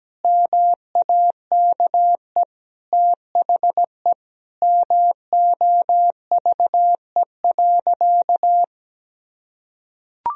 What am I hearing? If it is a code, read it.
Morse code, heard as a keyed tone, about 17 wpm, 700 Hz. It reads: MAKE THE MOVE.
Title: Sentences from Top 1000 Words - Review Part 1 - 17wpm